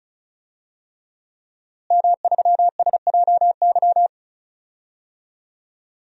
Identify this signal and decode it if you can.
Morse code, heard as a keyed tone, about 34 wpm, 700 Hz. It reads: M3SJY
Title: DXCC-100 Call Signs - 5 Characters - Repeated 3x - 35wpm